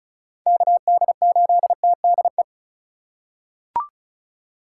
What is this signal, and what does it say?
Morse code, heard as a keyed tone, about 36 wpm, 700 Hz. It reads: KD8TDE
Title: USA Call Signs - 4x Character Spacing - 35wpm